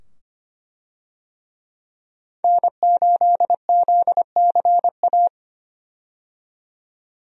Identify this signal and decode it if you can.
Morse code, heard as a keyed tone, about 25 wpm, 700 Hz. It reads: N8ZCA
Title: USA Call Signs - 5 Characters - 25wpm